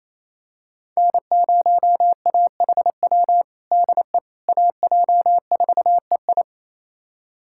Morse code, heard as a keyed tone, about 28 wpm, 700 Hz. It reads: N0AHW DE AJ4EI
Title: USA Callsign DE Callsign - 5 Characters - 28wpm